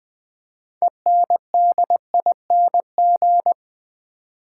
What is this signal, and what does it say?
Morse code, heard as a keyed tone, about 20 wpm, 700 Hz. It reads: ENDING
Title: 6 Letter Words - First 3 Letters Given - 20wpm